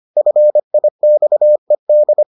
Morse code, heard as a keyed tone, about 25 wpm, 600 Hz. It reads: FIXED